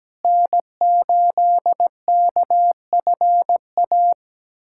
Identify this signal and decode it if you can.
Morse code, heard as a keyed tone, about 17 wpm, 700 Hz. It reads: N8KFA
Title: USA Callsign DE Callsign - 5 Characters - 17wpm